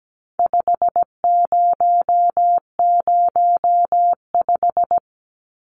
Morse code, heard as a keyed tone, about 17 wpm, 700 Hz. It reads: 5005
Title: Numbers - 4-Digit 17wpm